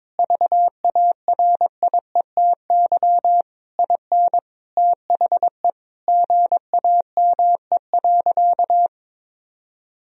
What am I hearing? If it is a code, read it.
Morse code, heard as a keyed tone, about 22 wpm, 700 Hz. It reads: VARIETY IN THE GAME.
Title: Sentences from Top 1000 Words - Review Part 4 - 22wpm